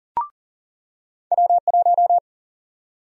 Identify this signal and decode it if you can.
Morse code, heard as a keyed tone, about 40 wpm, 700 Hz. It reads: W1
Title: USA Call Sign Prefixes - 40wpm